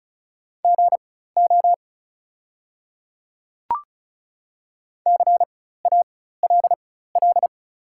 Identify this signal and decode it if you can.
Morse code, heard as a keyed tone, about 34 wpm, spaced as for 16 wpm, 690 Hz. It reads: GO CALL
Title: 4x Character Spacing - Top 100 Words - 35wpm